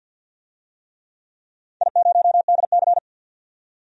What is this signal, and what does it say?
Morse code, heard as a keyed tone, about 53 wpm, 700 Hz. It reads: I0DC
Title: DXCC-100 Call Signs - 4 Characters - 50wpm